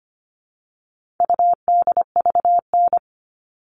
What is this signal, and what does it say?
Morse code, heard as a keyed tone, about 25 wpm, 700 Hz. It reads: UDVN
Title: All Cut Numbers - 4-Digit 25wpm